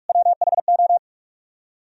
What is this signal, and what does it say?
Morse code, heard as a keyed tone, about 61 wpm, spaced as for 53 wpm, 700 Hz. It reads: WHO
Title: Speed-Racing - Sets of 3 Words 45wpm